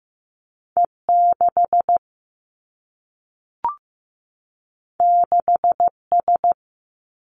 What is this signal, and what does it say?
Morse code, heard as a keyed tone, about 15 wpm, 700 Hz. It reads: E 6 6 S
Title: Two Challenging Letter-Number Combinations 15wpm